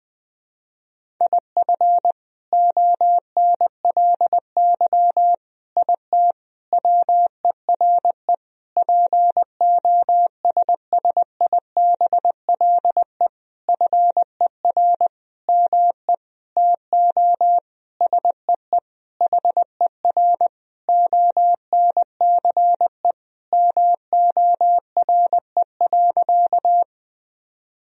Morse code, heard as a keyed tone, about 20 wpm, 700 Hz. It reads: IF ONLY IT WERE POSSIBLE FER ME TO SEE HER ONCE MORE.
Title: Sentences from Top 500 Words and Abbreviations 20wpm